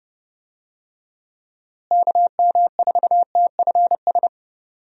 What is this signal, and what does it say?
Morse code, heard as a keyed tone, about 30 wpm, 700 Hz. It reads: KM4TFS
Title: USA Call Signs - 6 Characters - Repeated 3x - 30wpm